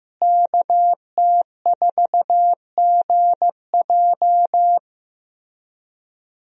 Morse code, heard as a keyed tone, about 16 wpm, 700 Hz. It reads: KT4GJ
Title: USA Call Signs - 5 Characters - Repeated 2x - 15wpm